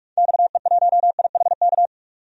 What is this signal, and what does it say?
Morse code, heard as a keyed tone, about 46 wpm, 700 Hz. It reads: XE1IHK
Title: DXCC-100 Call Signs - 6 Characters - 45wpm